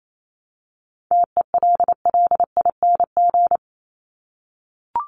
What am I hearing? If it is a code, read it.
Morse code, heard as a keyed tone, about 28 wpm, 700 Hz. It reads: TELLING
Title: ING Suffix - Top 100 Words with Base Words 28wpm